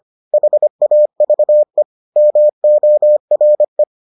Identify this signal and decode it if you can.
Morse code, heard as a keyed tone, about 25 wpm, 600 Hz. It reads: HAVE MORE